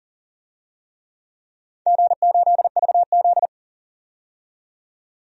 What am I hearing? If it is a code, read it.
Morse code, heard as a keyed tone, about 42 wpm, 700 Hz. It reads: G8VZ
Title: DXCC-100 Call Signs - 4 Characters - 40wpm